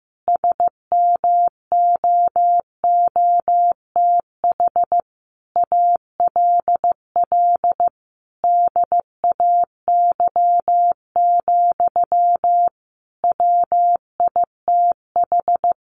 Morse code, heard as a keyed tone, about 15 wpm, 700 Hz. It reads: SMOOTH ALL DAY, WITH
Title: Sentences from Top 1400 Words 15wpm